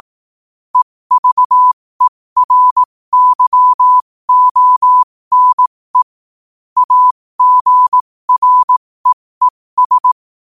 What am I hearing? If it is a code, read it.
Morse code, heard as a keyed tone, about 18 wpm, spaced as for 15 wpm, 1000 Hz. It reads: EVERYONE AGREES
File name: qotdfiles / 15wpm